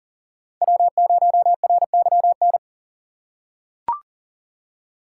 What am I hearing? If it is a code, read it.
Morse code, heard as a keyed tone, about 40 wpm, 700 Hz. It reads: W0RYN
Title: USA Call Signs - 5 Characters - 2x Character Spacing - 40wpm